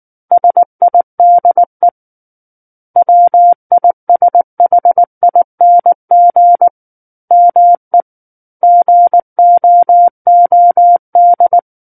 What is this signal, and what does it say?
Morse code, heard as a keyed tone, about 19 wpm, 700 Hz. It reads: SIDE WISHING ME GOOD
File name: Book One / 10-Chapter 10 - 19 WPM